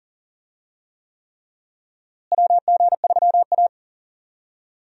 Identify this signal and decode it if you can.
Morse code, heard as a keyed tone, about 40 wpm, 700 Hz. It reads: WG3A
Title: USA Call Signs - 4 Characters - Repeated 3x - 40wpm